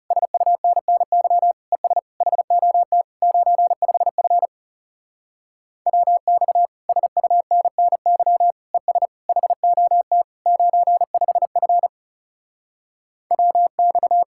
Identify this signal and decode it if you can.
Morse code, heard as a keyed tone, about 38 wpm, 700 Hz. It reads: SUNNY ES HOT 95F WX SUNNY ES HOT 95F WX